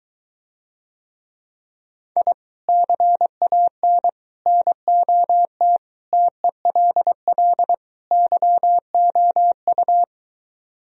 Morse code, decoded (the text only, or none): I CAN NOT TELL YOU